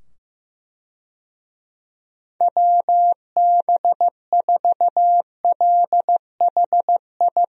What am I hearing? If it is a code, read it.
Morse code, heard as a keyed tone, about 15 wpm, 700 Hz. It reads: WB4LHI
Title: USA Call Signs - 6 Characters - 15wpm